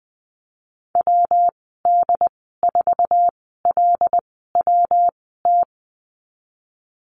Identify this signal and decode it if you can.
Morse code, heard as a keyed tone, about 19 wpm, spaced as for 14 wpm, 700 Hz. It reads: WD4LWT
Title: 173 Lesson 28g - Number 9 - Call Signs 20wpm